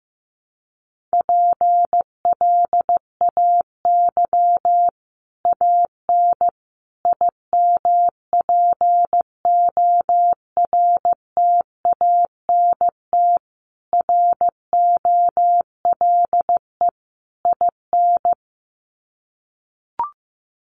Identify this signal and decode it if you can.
Morse code, heard as a keyed tone, about 15 wpm, 700 Hz. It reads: PLAY AN IMPORTANT ROLE IN
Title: Sets of 5 Words Encore 15wpm